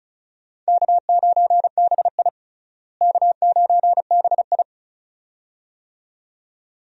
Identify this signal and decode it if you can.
Morse code, heard as a keyed tone, about 36 wpm, 700 Hz. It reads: K9BI K9BI
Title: USA Call Signs - 4 Characters - Repeated 2x - 35wpm